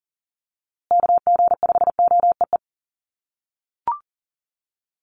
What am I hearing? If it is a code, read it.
Morse code, heard as a keyed tone, about 40 wpm, 700 Hz. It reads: KG5OEE
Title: USA Call Signs Repeated 2x 40wpm